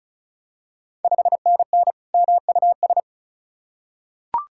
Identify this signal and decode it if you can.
Morse code, heard as a keyed tone, about 36 wpm, 700 Hz. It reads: 5NN MUS
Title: Ontario QSO Party Exchange 35wpm